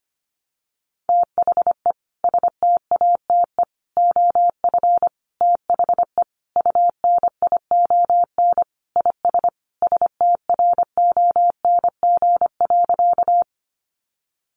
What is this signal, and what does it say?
Morse code, heard as a keyed tone, about 25 wpm, 700 Hz. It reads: THE STATE OF THE UNION IS STRONG.
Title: Sentences from Top 1400 Words 25wpm